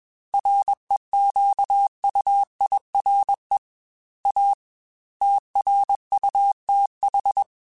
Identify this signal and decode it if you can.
Morse code, heard as a keyed tone, about 21 wpm, 800 Hz. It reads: REQUIRE A TRUTH
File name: Technical Subjects / Technical Subjects-mp3s / AND Gate @20